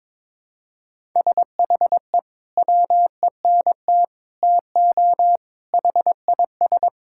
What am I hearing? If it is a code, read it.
Morse code, heard as a keyed tone, about 22 wpm, 700 Hz. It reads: SHE WENT TO HIS
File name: practice / 22wpm / Sentences with 2-Word Phrases - 22wpm